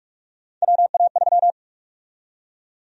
Morse code, heard as a keyed tone, about 46 wpm, 700 Hz. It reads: WA3
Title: USA Call Sign Prefixes - 45wpm